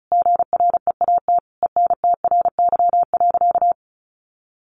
Morse code, heard as a keyed tone, about 34 wpm, 700 Hz. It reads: GREAT ENTRY.
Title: Sentences from Top 1500 Words 35wpm